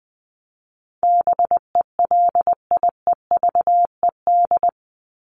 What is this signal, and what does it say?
Morse code, heard as a keyed tone, about 20 wpm, 700 Hz. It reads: BELIEVED
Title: ED Suffix - Top 100 Words with Base Words 20wpm